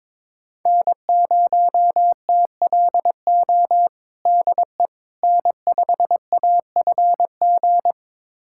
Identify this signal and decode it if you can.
Morse code, heard as a keyed tone, about 22 wpm, 700 Hz. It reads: N0TLO DE N5AFG
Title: USA Callsign DE Callsign - 5 Characters - 22wpm